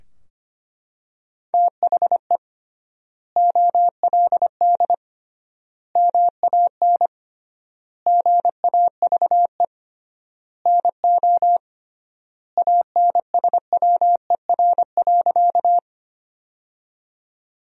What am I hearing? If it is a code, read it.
Morse code, heard as a keyed tone, about 25 wpm, 700 Hz. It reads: THE OLD MAN GAVE NO ANSWER.
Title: 3x Word-Spacing - Sentences from Top 500 Words - 25wpm